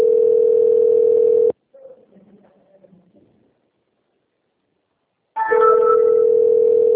Ringback tone (North America), 440 + 480 Hz.